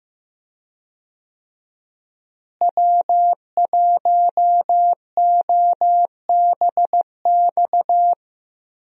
Morse code, heard as a keyed tone, about 15 wpm, 700 Hz. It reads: W1OBX
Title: USA Call Signs - 5 Characters - Repeated 3x - 15wpm